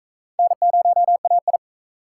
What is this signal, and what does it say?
Morse code, heard as a keyed tone, about 39 wpm, 700 Hz. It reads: N0AI